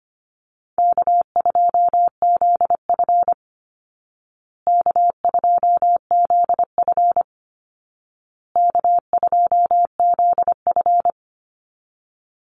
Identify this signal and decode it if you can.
Morse code, heard as a keyed tone, about 25 wpm, 700 Hz. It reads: K2ZF K2ZF K2ZF
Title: Easy Call Signs Repeated 3x - 25wpm